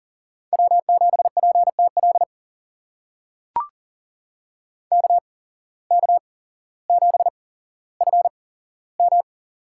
Morse code, heard as a keyed tone, about 40 wpm, spaced as for 17 wpm, 700 Hz. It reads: W7PTL KK7FM
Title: USA Call Signs - 5 Characters - 8x Character Spacing - 40wpm